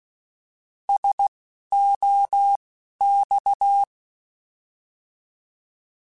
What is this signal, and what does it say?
Morse code, heard as a keyed tone, about 16 wpm, spaced as for 11 wpm, 800 Hz. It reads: SOX
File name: Alphabetical / Alphabetical mp3s / Ss @10